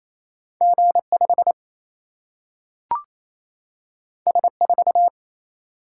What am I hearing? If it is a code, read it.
Morse code, heard as a keyed tone, about 28 wpm, 700 Hz. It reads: G5 S4
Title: Single-Letter Single-Number 28wpm